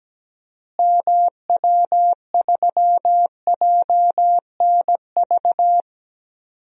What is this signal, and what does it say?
Morse code, heard as a keyed tone, about 17 wpm, 700 Hz. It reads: MW3JNV
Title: DXCC-100 Call Signs - 6 Characters - Repeated 3x - 17wpm